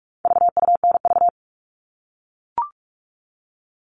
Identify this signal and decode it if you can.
Morse code, heard as a keyed tone, about 44 wpm, 700 Hz. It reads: VUNV
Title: All Cut Numbers - 4-Digit 45wpm